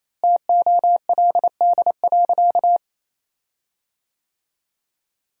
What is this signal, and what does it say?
Morse code, heard as a keyed tone, about 28 wpm, 700 Hz. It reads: TOLD.
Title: 4x Word-Spacing - Sentences from Top 400 Words - 28wpm